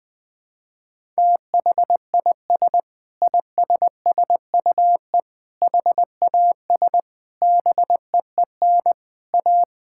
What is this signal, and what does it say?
Morse code, heard as a keyed tone, about 20 wpm, 700 Hz. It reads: THIS ISSUE HAS BEEN A
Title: Sentences with 8-Word Phrases - 20wpm